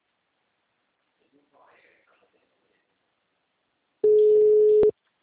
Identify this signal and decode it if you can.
Ringback tone (Europe, 425 Hz), with speech.
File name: out- -2007- - - .12096